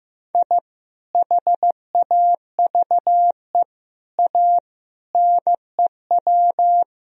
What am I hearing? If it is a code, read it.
Morse code, heard as a keyed tone, about 15 wpm, 700 Hz. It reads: I HAVE A NEW
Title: Sentences from Top 1000 Words - Review Part 3 - 15wpm